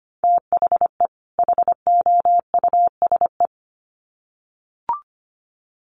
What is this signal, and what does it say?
Morse code, heard as a keyed tone, about 25 wpm, 700 Hz. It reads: THE HOUSE
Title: Sets of 2 Words 25wpm